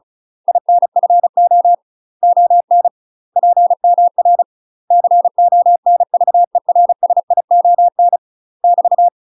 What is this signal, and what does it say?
Morse code, heard as a keyed tone, about 34 wpm, 700 Hz. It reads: INFO ON PMR CONVERSION <BT>
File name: UK MASTER QSOs TRAINING / 35WPM / UK MASTER QSOs 0005